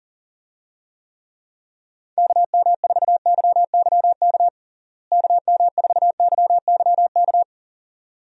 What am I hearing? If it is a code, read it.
Morse code, heard as a keyed tone, about 40 wpm, 700 Hz. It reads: KM4YYK KM4YYK
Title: USA Call Signs - 6 Characters - Repeated 2x - 40wpm